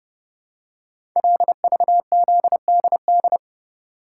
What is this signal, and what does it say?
Morse code, heard as a keyed tone, about 30 wpm, 700 Hz. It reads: LVZDD
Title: Code Groups - 5 Letters - 30wpm